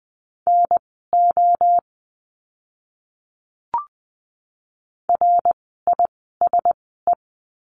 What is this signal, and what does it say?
Morse code, heard as a keyed tone, about 20 wpm, spaced as for 14 wpm, 700 Hz. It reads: NO RISE
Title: 054 Lesson 10f - Letter r - Words 20wpm